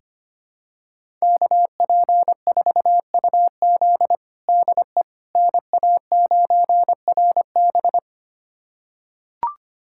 Morse code, heard as a keyed tone, about 25 wpm, 700 Hz. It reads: KP4UZ DE NA9RB
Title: USA Callsign DE Callsign - 5 Characters - 25wpm